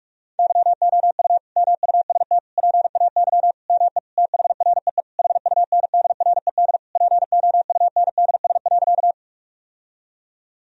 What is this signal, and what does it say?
Morse code, heard as a keyed tone, about 46 wpm, 700 Hz. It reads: YOU MUST PAY ME THREE HUNDRED POUNDS.
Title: Sentences from Top 1000 Words - Review Part 2 - 45wpm